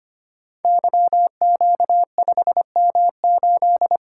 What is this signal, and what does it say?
Morse code, heard as a keyed tone, about 25 wpm, 700 Hz. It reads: YQ5M8
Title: Code Groups - 5 Letter:Numbers - 25wpm